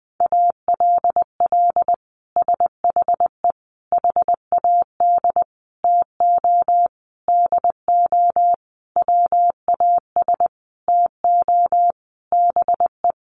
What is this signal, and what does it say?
Morse code, heard as a keyed tone, about 20 wpm, 700 Hz. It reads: ALL SHE HAD TO DO WAS TO BE
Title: Sentences from Top 1100 Words 20wpm